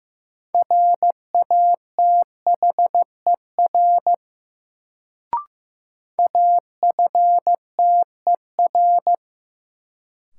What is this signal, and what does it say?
Morse code, heard as a keyed tone, about 15 wpm, 700 Hz. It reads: RATHER AFTER
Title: Sets of 2 Words - New Word Drill - 15wpm